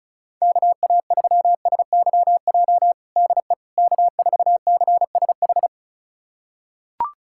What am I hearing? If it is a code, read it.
Morse code, heard as a keyed tone, about 36 wpm, 700 Hz. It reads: KA3SYJ DE K4CSH
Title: USA Callsign DE Callsign 35wpm